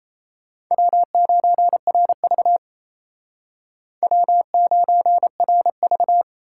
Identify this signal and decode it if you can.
Morse code, heard as a keyed tone, about 35 wpm, 700 Hz. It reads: W9RV W9RV
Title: Speed Racing - USA 1x2 Call Signs 28wpm